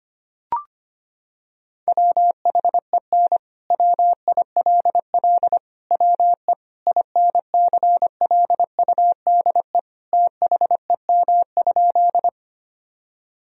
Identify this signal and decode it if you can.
Morse code, heard as a keyed tone, about 25 wpm, 700 Hz. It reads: WHEN WILL WE INCLUDE THEM?
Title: Sentences from Top 1000 Words - Review Part 3 - 25wpm